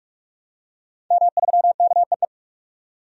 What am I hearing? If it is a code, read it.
Morse code, heard as a keyed tone, about 42 wpm, 700 Hz. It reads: M3KEE
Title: DXCC-100 Call Signs - 5 Characters - 45wpm